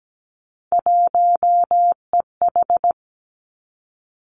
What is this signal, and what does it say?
Morse code, heard as a keyed tone, about 17 wpm, 700 Hz. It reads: 1EH
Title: Three Challenging Letter-Number Combinations 17wpm